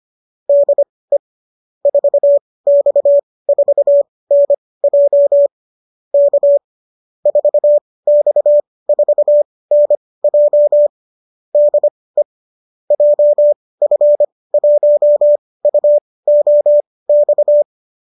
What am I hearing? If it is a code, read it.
Morse code, heard as a keyed tone, about 25 wpm, spaced as for 18 wpm, 590 Hz. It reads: DE 4X4NJ K 4X4NJ DE JF1UOX